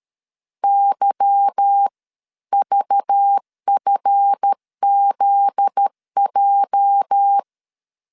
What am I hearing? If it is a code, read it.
Morse code, heard as a keyed tone, about 13 wpm, spaced as for 11 wpm, 800 Hz. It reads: YVFZJ
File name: CW / DET / 1 9 1500C S C 12wpm 3811